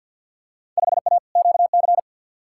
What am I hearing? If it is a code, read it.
Morse code, heard as a keyed tone, about 54 wpm, 700 Hz. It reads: 5A QC
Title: Field Day Exchanges Encore - 50wpm